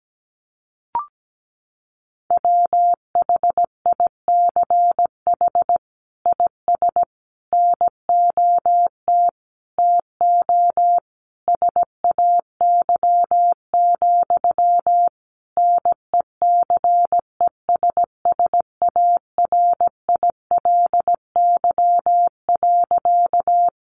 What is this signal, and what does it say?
Morse code, heard as a keyed tone, about 17 wpm, 700 Hz. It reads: WHICH IS NOT TO SAY, NECESSARILY.
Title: Sentences from Top 1400 Words 17wpm